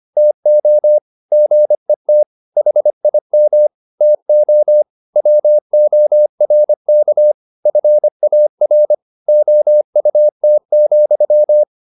Morse code, heard as a keyed tone, about 25 wpm, 600 Hz. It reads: TO GET HIM TO WORK FAR OUT,
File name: the old man and the sea part 1 25wpm